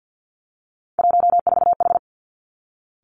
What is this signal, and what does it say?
Morse code, heard as a keyed tone, about 50 wpm, 700 Hz. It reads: 14H
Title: Three Challenging Letter-Number Combinations 50wpm